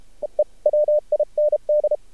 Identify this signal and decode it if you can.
Morse code, heard as a keyed tone, about 32 wpm, 600 Hz. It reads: EE WIND
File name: archived / CRSnet